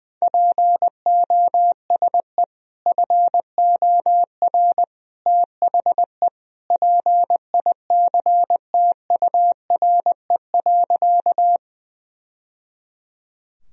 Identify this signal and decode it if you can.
Morse code, heard as a keyed tone, about 21 wpm, 700 Hz. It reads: POSE FOR THE PICTURE.
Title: Sentences from Top 1000 Words - Review Part 2 - 20wpm